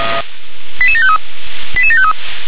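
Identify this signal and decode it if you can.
Five-tone selective call (ZVEI), heard as quick runs of five tones, tones of about 70 ms. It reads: 88753 88753
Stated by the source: ZVEI five-tone selective call 88768 received by SDR.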